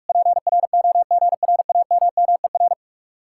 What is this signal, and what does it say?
Morse code, heard as a keyed tone, about 57 wpm, 700 Hz. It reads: PROGRAMMER